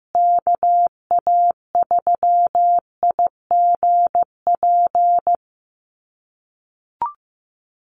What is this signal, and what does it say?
Morse code, heard as a keyed tone, about 15 wpm, 700 Hz. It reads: KA3IGP
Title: USA Call Signs Repeated 2x 15wpm